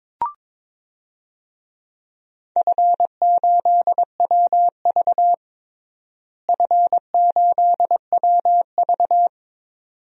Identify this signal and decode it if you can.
Morse code, heard as a keyed tone, about 22 wpm, 700 Hz. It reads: F8WV F8WV
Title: DXCC-100 Call Signs - 4 Characters - Repeated 2x - 22wpm